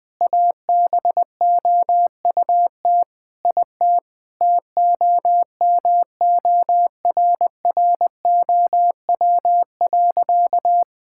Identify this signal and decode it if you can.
Morse code, heard as a keyed tone, about 19 wpm, 700 Hz. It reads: ABOUT IT TOMORROW.
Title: Sentences with 2-Word Phrases - 20wpm